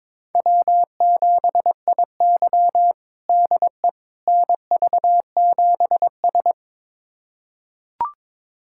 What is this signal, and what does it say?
Morse code, heard as a keyed tone, about 22 wpm, 700 Hz. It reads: W7IY DE NV7S
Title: USA Callsign DE Callsign - 4 Characters - 22wpm